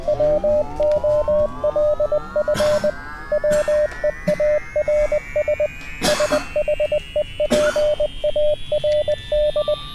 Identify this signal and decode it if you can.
Morse code, heard as a keyed tone, about 20 wpm, 600 Hz. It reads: WOLF WEARS SHEPARD